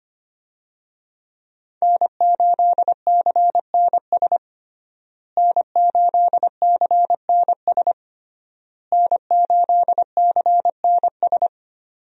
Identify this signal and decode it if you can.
Morse code, heard as a keyed tone, about 25 wpm, 700 Hz. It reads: N8CNS N8CNS N8CNS
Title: USA Call Signs - 5 Characters - Repeated 3x - 25wpm